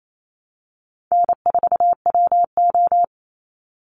Morse code, heard as a keyed tone, about 28 wpm, 700 Hz. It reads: N4WO
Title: Easy Call Signs - 28wpm